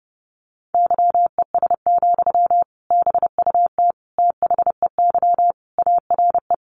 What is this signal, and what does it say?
Morse code, heard as a keyed tone, about 30 wpm, 700 Hz. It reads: YES, BUT THEY ARE
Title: Sentences from Top 1400 Words 30wpm